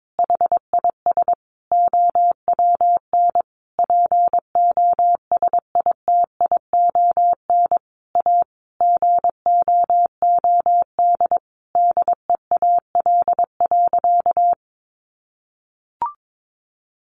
Morse code, heard as a keyed tone, about 22 wpm, 700 Hz. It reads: HIS OWN POSITION A GOOD DEAL.
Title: Sentences from Top 1200 Words 22wpm